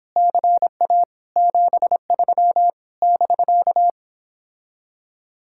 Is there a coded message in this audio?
Yes: Morse code from 0.2 to 3.9 s, heard as a keyed tone, about 29 wpm, 700 Hz. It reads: CA 73 <BK>